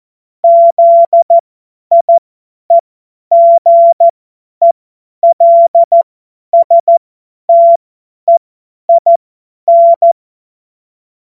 Morse code, heard as a keyed tone, about 14 wpm, spaced as for 10 wpm, 680 Hz. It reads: ZIEGELSTEIN
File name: Morsekurs / Witze / Witz 18. Januar 2022, 50 F70